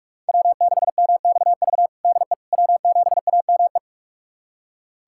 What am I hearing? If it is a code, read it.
Morse code, heard as a keyed tone, about 46 wpm, 700 Hz. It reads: W6MXV DE W7AME
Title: USA Callsign DE Callsign - 5 Characters - 45wpm